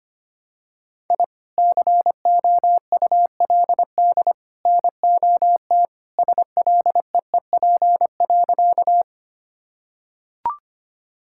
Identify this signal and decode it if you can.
Morse code, heard as a keyed tone, about 25 wpm, 700 Hz. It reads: I COULD NOT SLEEP.
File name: practice / 25wpm / Sentences with 3-Word Phrases - 25wpm